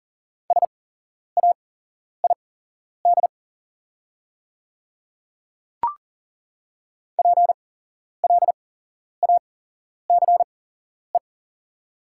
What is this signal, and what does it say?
Morse code, heard as a keyed tone, about 40 wpm, spaced as for 11 wpm, 690 Hz. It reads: SAID PLACE
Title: 8x Character Spacing - Top 200 Words - 40wpm